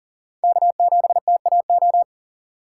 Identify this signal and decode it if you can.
Morse code, heard as a keyed tone, about 40 wpm, 700 Hz. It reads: K7TAO
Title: USA Call Signs - 5 Characters - 4x Character Spacing - 40wpm